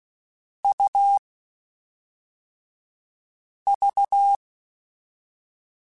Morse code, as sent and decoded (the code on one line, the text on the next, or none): ..- / ...-
U V